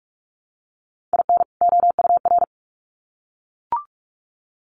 Morse code, heard as a keyed tone, about 44 wpm, 700 Hz. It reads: IN OUR
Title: Sets of 2 Words 45wpm